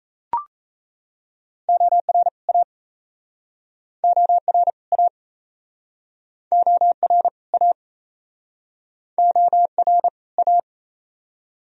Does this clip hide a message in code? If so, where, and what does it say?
Yes: Morse code from 1.7 to 10.6 s, heard as a keyed tone, about 40 wpm, 700 Hz. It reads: OR A OR A OR A OR A